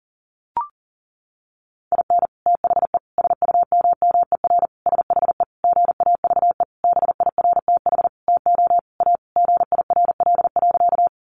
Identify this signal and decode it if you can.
Morse code, heard as a keyed tone, about 40 wpm, 700 Hz. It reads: IN THE SUMMER SHE GAVE BIRTH TO A GIRL.
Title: Sentences from Top 1100 Words 40wpm